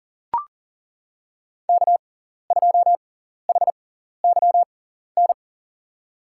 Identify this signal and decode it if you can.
Morse code, heard as a keyed tone, about 40 wpm, spaced as for 14 wpm, 700 Hz. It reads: K2HYN
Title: USA Call Signs - 5 Characters - 6x Character Spacing - 40wpm